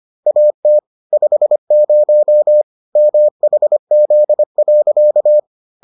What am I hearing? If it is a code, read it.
Morse code, heard as a keyed tone, about 25 wpm, 600 Hz. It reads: AT 50 MHZ.